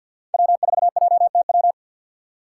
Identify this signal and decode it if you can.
Morse code, heard as a keyed tone, about 51 wpm, 700 Hz. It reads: W4JTW
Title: USA Call Signs - 5 Characters - 2x Character Spacing - 50wpm